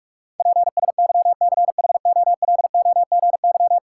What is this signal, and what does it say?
Morse code, heard as a keyed tone, about 41 wpm, 700 Hz. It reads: PSYCHOLOGY
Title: Sentences from Top 1400 Words - New Word Drill Alt - 45wpm